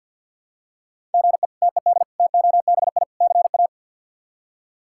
Morse code, heard as a keyed tone, about 52 wpm, 700 Hz. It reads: GE TED TOBI CA
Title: K1USN SST QSO Practice - 50wpm